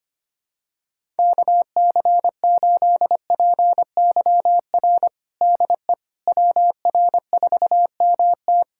Morse code, heard as a keyed tone, about 25 wpm, 700 Hz. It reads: KC8PYR DE WR4MT
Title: USA Callsign DE Callsign 25wpm